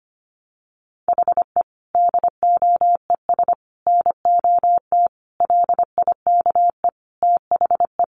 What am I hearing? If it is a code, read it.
Morse code, heard as a keyed tone, about 25 wpm, 700 Hz. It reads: HE DOES NOT LIKE THE
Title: Sentences from Top 1200 Words 25wpm